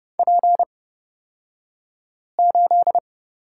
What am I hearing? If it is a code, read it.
Morse code, heard as a keyed tone, about 31 wpm, spaced as for 5 wpm, 700 Hz. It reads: P8
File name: Lesson 36d - BT - Two Characters 30wpm